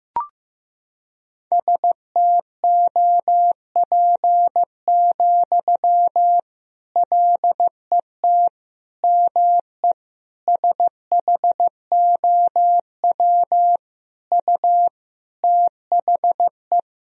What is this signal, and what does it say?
Morse code, heard as a keyed tone, about 16 wpm, 700 Hz. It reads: STOP, LET ME SHOW U THE
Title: Sentences from Top 300 Words and Abbreviations 15wpm